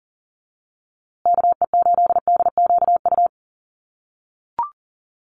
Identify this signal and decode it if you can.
Morse code, heard as a keyed tone, about 40 wpm, 700 Hz. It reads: KE8DQU
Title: USA Call Signs Repeated 3x 40wpm